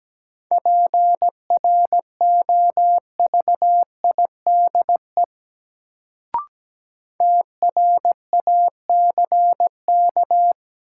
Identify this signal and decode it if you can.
Morse code, heard as a keyed tone, about 17 wpm, 700 Hz. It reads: PROVIDE TRACK
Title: Sentences from Top 1100 Words - New Word Drill Alt - 17wpm